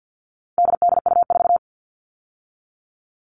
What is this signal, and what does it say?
Morse code, heard as a keyed tone, about 50 wpm, 700 Hz. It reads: DDU4
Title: All Cut Numbers - 4-Digit 50wpm